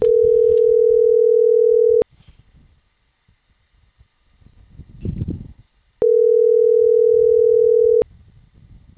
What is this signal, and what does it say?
Ringback tone (North America), 440 + 480 Hz.